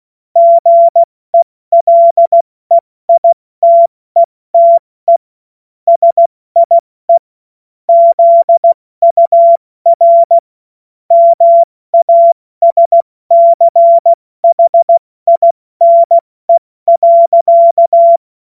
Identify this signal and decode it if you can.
Morse code, heard as a keyed tone, about 16 wpm, 680 Hz. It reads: GELEITETE SIE ZUR MASCHINE.
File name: Morsekurs / Witze / Witz 16 Januar 2024 70 F80